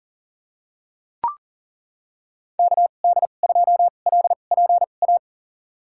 Morse code, heard as a keyed tone, about 38 wpm, spaced as for 28 wpm, 700 Hz. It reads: KD2LPA
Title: USA Call Signs - 6 Characters - 2x Character Spacing - 40wpm